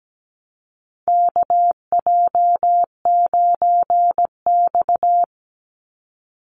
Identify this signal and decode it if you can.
Morse code, heard as a keyed tone, about 17 wpm, 700 Hz. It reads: KJ9X
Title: NA SOTA Chaser Callsigns 17wpm